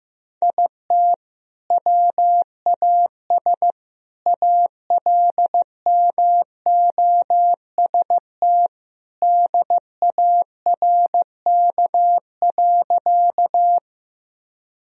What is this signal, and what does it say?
Morse code, heard as a keyed tone, about 16 wpm, 700 Hz. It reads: IT WAS ALMOST DARK.
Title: Sentences from Top 1000 Words - Review Part 4 - 15wpm